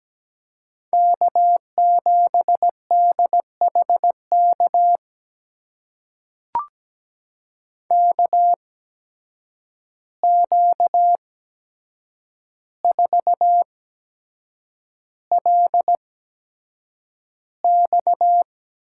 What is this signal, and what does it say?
Morse code, heard as a keyed tone, about 17 wpm, spaced as for 7 wpm, 700 Hz. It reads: K7DHK KQ4LX
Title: USA Call Signs - 5 Characters - 8x Character Spacing - 17wpm